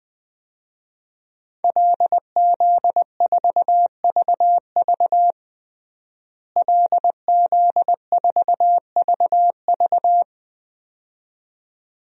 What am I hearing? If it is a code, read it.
Morse code, heard as a keyed tone, about 20 wpm, 700 Hz. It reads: LZ4VV LZ4VV
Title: DXCC-100 Call Signs - 5 Characters - Repeated 2x - 20wpm